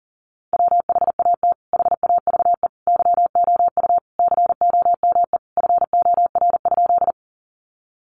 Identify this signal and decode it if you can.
Morse code, heard as a keyed tone, about 40 wpm, 700 Hz. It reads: WHAT HAVE YOU COME FOR?
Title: Sentences from Top 100 Words 40wpm